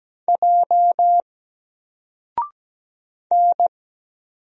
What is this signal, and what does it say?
Morse code, heard as a keyed tone, about 17 wpm, 700 Hz. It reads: J N
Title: Single Character (Letter:Number:Punctuation:Prosign) 17wpm